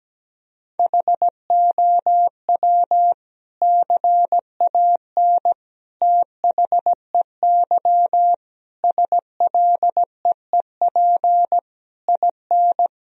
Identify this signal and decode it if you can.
Morse code, heard as a keyed tone, about 17 wpm, 700 Hz. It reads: HOW CAN THEY SLEEP IN
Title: Sentences from Top 1000 Words - Review Part 3 - 17wpm